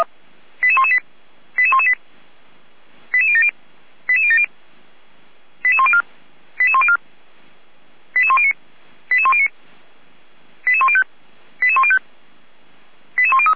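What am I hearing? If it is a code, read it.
Five-tone selective call (ZVEI), heard as quick runs of five tones, tones of about 80 ms. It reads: 88118 88118 88070 88070 88104 88104 88109 88109 88105 88105 88103
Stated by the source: ZVEI five-tone call received by SDR, station feuerwehr-orlamuende.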